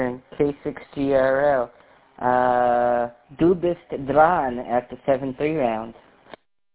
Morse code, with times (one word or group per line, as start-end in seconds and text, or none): none